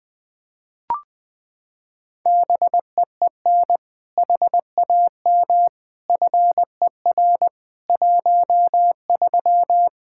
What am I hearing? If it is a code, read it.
Morse code, heard as a keyed tone, about 20 wpm, 700 Hz. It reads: BEEN HAM FER 13
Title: Practice QSO - Exchange 3 - More Info - 20wpm